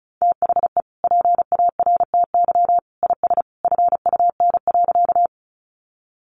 Morse code, heard as a keyed tone, about 34 wpm, 700 Hz. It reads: THE PARTY IS FUN.
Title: Easy Phrases 35wpm